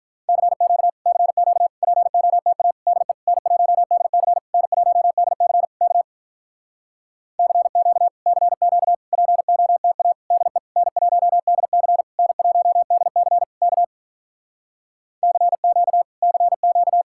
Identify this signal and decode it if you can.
Morse code, heard as a keyed tone, about 48 wpm, 700 Hz. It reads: CQ CQ POTA DE N1DC N1DC K CQ CQ POTA DE N1DC N1DC K CQ CQ